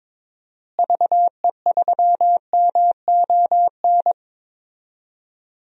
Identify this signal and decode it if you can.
Morse code, heard as a keyed tone, about 22 wpm, 700 Hz. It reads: VE3MON
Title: DXCC-100 Call Signs - 6 Characters - Repeated 3x - 22wpm